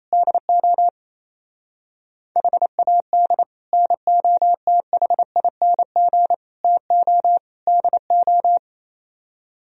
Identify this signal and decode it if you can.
Morse code, heard as a keyed tone, about 26 wpm, 700 Hz. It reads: DO HAD NOTHING TO DO
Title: Speed-Racing - Sets of 4 Words 28wpm